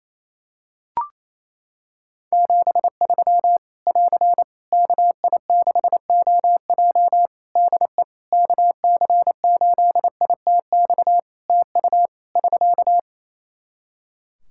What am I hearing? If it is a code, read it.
Morse code, heard as a keyed tone, about 28 wpm, 700 Hz. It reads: 73 <AR> KI6OJ DE KC8ITX TU <SK>
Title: K1USN SST QSO Practice 28wpm